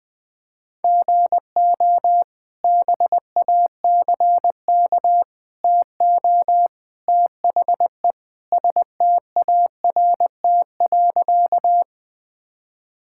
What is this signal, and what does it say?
Morse code, heard as a keyed tone, about 20 wpm, 700 Hz. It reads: GO BACK TO THE START.
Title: Sentences with 3-Word Phrases - 20wpm